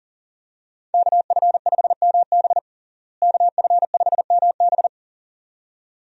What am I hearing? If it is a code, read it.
Morse code, heard as a keyed tone, about 40 wpm, 700 Hz. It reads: KF5MB KF5MB
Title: USA Call Signs - 5 Characters - Repeated 2x - 40wpm